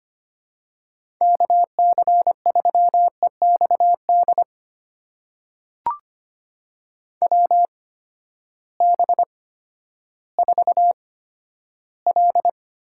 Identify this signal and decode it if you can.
Morse code, heard as a keyed tone, about 25 wpm, spaced as for 12 wpm, 700 Hz. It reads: KC3EXD WB4L
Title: USA Call Signs - 6 Characters - 8x Character Spacing - 25wpm